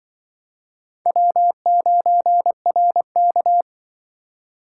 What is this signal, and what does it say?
Morse code, heard as a keyed tone, about 24 wpm, 700 Hz. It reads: W9RK
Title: Speed Racing - USA 1x2 Call Signs 20wpm